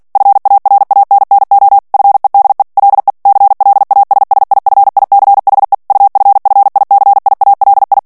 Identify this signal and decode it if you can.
Morse code, heard as a keyed tone, about 48 wpm, 800 Hz. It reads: FARANNO FEDE LE CLASSIFICHE UFFICIALI
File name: ITALIEN texte clair 2 vitesse 45 mots minute